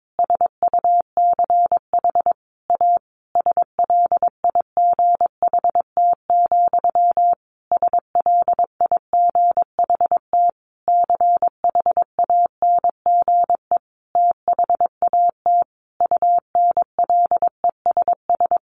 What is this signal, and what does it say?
Morse code, heard as a keyed tone, about 22 wpm, 700 Hz. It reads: SUCH A SLIGHT, SLIGHT CHANGE THAT UNLESS
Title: A Sound Of Thunder 22wpm